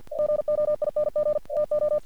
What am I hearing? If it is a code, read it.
Morse code, heard as a keyed tone, about 50 wpm, 600 Hz. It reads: GOING TO
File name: archived / CRSnet